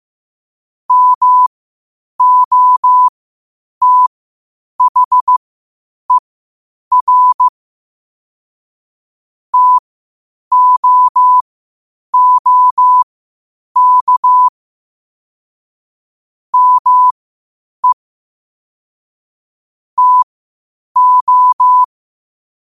Morse code, heard as a keyed tone, about 15 wpm, spaced as for 8 wpm, 1000 Hz. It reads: MOTHER TOOK ME TO
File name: qotdfiles / 8wpm